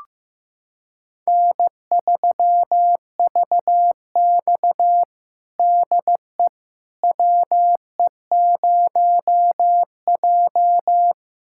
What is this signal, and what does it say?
Morse code, heard as a keyed tone, about 15 wpm, 700 Hz. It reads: N3VX DE WE0J
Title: USA Callsign DE Callsign - 4 Characters - 15wpm